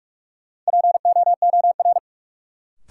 Morse code, heard as a keyed tone, about 47 wpm, 700 Hz. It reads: POOR
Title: Sentences from Top 1000 Words - New Word Drill - 45wpm